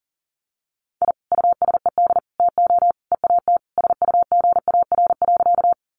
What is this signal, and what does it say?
Morse code, heard as a keyed tone, about 40 wpm, 700 Hz. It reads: I USED TO EAT SUGAR.
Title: Sentences from Top 1200 Words 40wpm